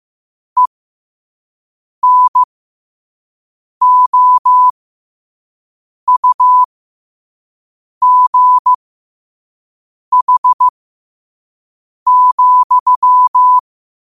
Morse code, heard as a keyed tone, about 15 wpm, spaced as for 5 wpm, 1000 Hz. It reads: ENOUGH,
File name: qotdfiles / 5wpm